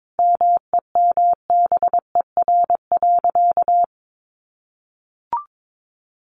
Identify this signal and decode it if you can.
Morse code, heard as a keyed tone, about 22 wpm, 700 Hz. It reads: MEMBER.
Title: Sentences from Top 1200 Words 22wpm